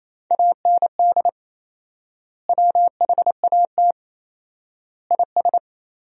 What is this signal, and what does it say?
Morse code, heard as a keyed tone, about 28 wpm, 700 Hz. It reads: AND WHAT IS
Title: 4x Word-Spacing - Sentences from Top 100 Words - 28wpm